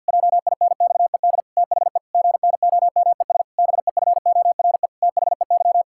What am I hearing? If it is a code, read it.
Morse code, heard as a keyed tone, about 52 wpm, 700 Hz. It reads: JINXED THE GNOMES BEFORE THEY
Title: Pangram Challenge v2 - 50wpm